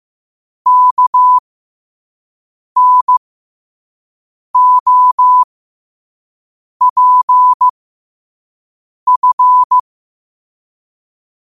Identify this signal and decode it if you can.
Morse code, heard as a keyed tone, about 15 wpm, spaced as for 5 wpm, 1000 Hz. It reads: KNOPF
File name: qotdfiles / 5wpm